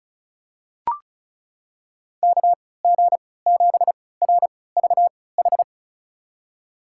Morse code, heard as a keyed tone, about 36 wpm, spaced as for 20 wpm, 700 Hz. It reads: KG7RVH
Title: USA Call Signs - 6 Characters - 3x Character Spacing - 35wpm